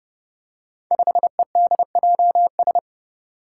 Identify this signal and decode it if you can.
Morse code, heard as a keyed tone, about 30 wpm, 700 Hz. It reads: 5EDJS
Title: Five Challenging Letter-Number Combinations 30wpm